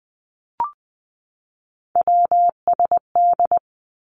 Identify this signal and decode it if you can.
Morse code, heard as a keyed tone, about 20 wpm, 700 Hz. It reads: WSD
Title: Three Challenging Letter-Number Combinations 20wpm